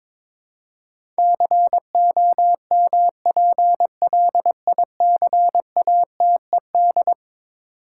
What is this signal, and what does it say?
Morse code, heard as a keyed tone, about 22 wpm, 700 Hz. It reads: COMPLICATED
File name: practice / 22wpm / Eleven Letter Words - 22wpm